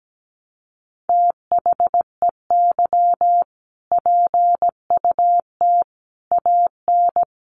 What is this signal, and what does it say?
Morse code, heard as a keyed tone, about 17 wpm, 700 Hz. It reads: THEY PUT AN
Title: Sentences from Top 1400 Words 17wpm